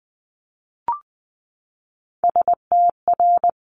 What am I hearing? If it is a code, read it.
Morse code, heard as a keyed tone, about 19 wpm, 700 Hz. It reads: STR
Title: Multi-letter Phonemes 20wpm